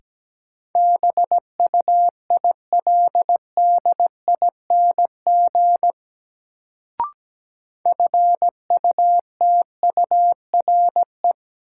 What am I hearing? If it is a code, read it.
Morse code, heard as a keyed tone, about 17 wpm, 700 Hz. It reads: BUILDING FUTURE
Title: Sentences from Top 700 Words - New Word Drill - 17wpm